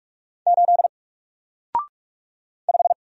Morse code, heard as a keyed tone, about 44 wpm, 700 Hz. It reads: 8 5
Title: Single Letter-Number Alt 45wpm